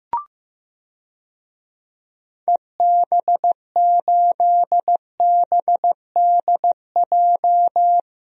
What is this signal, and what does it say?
Morse code, heard as a keyed tone, about 16 wpm, 700 Hz. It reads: EB8BDJ
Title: DXCC-100 Call Signs - 6 Characters - Repeated 3x - 15wpm